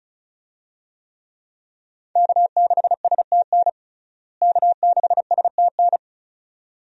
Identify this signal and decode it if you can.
Morse code, heard as a keyed tone, about 34 wpm, 700 Hz. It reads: K6STN K6STN
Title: USA Call Signs - 5 Characters - Repeated 2x - 35wpm